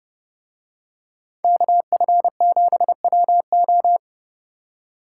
Morse code, heard as a keyed tone, about 28 wpm, 700 Hz. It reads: KF7WO